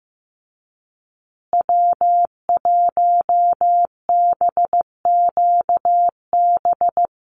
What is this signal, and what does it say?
Morse code, heard as a keyed tone, about 15 wpm, 700 Hz. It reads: W1BQB
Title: USA Call Signs Repeated 2x 15wpm